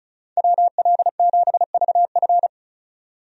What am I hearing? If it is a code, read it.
Morse code, heard as a keyed tone, about 36 wpm, 700 Hz. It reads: WL7VF
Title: USA Call Signs - 5 Characters - 35wpm